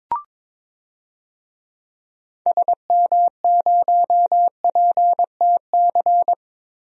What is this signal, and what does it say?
Morse code, heard as a keyed tone, about 22 wpm, 700 Hz. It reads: SM0PTC
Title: DXCC-100 Call Signs - 6 Characters - Repeated 3x - 22wpm